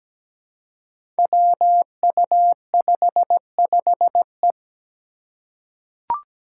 Morse code, heard as a keyed tone, about 17 wpm, 700 Hz. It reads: WU55E
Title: Five Challenging Letter-Number Combinations 17wpm